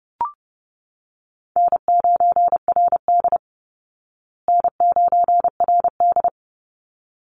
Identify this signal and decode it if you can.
Morse code, heard as a keyed tone, about 28 wpm, 700 Hz. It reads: N9RD N9RD
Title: Easy Call Signs Repeated 2x - 30wpm